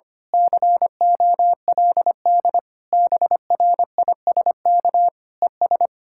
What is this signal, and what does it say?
Morse code, heard as a keyed tone, about 24 wpm, 700 Hz. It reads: COLD BRISK ES
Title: Speed-Racing Common QSO Phrases Encore - 25wpm